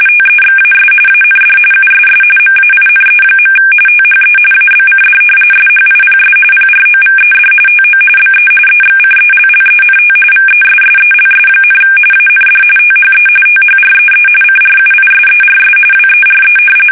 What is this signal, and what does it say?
RTTY, about 45.45 baud. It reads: TEXAS AND USS MAINE. THE ABCD SHIPS PROVED TO BE EXCELLENT VESSELS, AND THE THREE CRUISERS WERE
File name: fsk850 / usnavy3